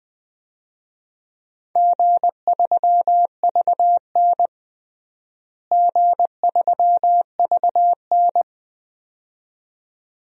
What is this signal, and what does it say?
Morse code, heard as a keyed tone, about 20 wpm, 700 Hz. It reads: G3VN G3VN
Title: DXCC-100 Call Signs - 4 Characters - Repeated 2x - 20wpm